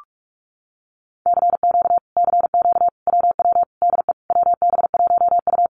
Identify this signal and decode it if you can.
Morse code, heard as a keyed tone, about 46 wpm, 700 Hz. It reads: CQ CQ WW DE WB1U
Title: Common QSO Phrases Encore 45wpm